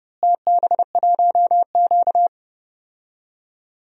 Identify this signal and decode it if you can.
Morse code, heard as a keyed tone, about 31 wpm, 700 Hz. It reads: TB1Q
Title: DXCC-100 Call Signs - 4 Characters - Repeated 3x - 30wpm